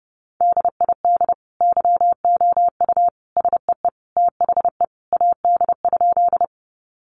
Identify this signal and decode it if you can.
Morse code, heard as a keyed tone, about 30 wpm, 700 Hz. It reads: DID YOU SEE THE AD?
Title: Sentences from Top 1300 Words 30wpm